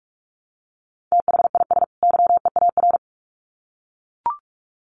Morse code, heard as a keyed tone, about 44 wpm, 700 Hz. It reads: THIS YEAR
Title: Sets of 2 Words 45wpm